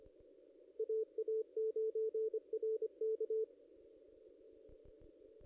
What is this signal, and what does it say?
Morse code, heard as a keyed tone, about 25 wpm, 440 Hz, with radio noise. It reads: AA9RK
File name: ss-cw-k5tr-op-n5rz